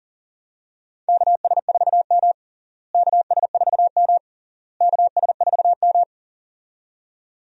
Morse code, heard as a keyed tone, about 40 wpm, 700 Hz. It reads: KS4M KS4M KS4M
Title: USA Call Signs - 4 Characters - Repeated 3x - 40wpm